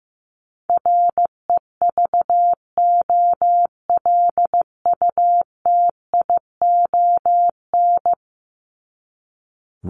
Morse code, heard as a keyed tone, about 15 wpm, 700 Hz. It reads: REVOLUTION
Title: Top 1751-2000 Words 15wpm